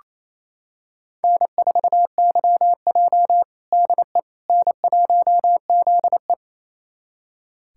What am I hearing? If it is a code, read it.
Morse code, heard as a keyed tone, about 28 wpm, 700 Hz. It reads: N4YJ DE N1ZE
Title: USA Callsign DE Callsign - 4 Characters - 28wpm